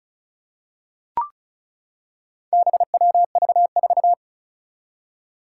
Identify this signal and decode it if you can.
Morse code, heard as a keyed tone, about 36 wpm, 700 Hz. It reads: BWV4
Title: Four Challenging Letter-Number Combinations 35wpm